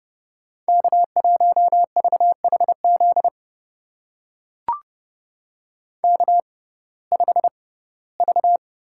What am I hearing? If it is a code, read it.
Morse code, heard as a keyed tone, about 30 wpm, spaced as for 18 wpm, 700 Hz. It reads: K1VHZ K5V
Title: USA Call Signs - 5 Characters - 6x Character Spacing - 30wpm